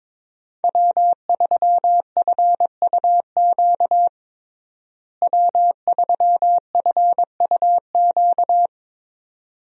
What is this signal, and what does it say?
Morse code, heard as a keyed tone, about 22 wpm, 700 Hz. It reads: W3FUQ W3FUQ
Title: USA Call Signs - 5 Characters - Repeated 2x - 22wpm